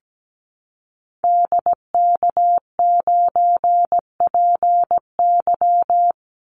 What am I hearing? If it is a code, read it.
Morse code, heard as a keyed tone, about 17 wpm, 700 Hz. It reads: DK9PY
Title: Call Signs Repeated 2x - 17wpm